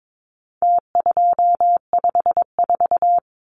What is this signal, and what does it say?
Morse code, heard as a keyed tone, about 22 wpm, 700 Hz. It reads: T254
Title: Numbers - Contest Style - 4-Digit 22wpm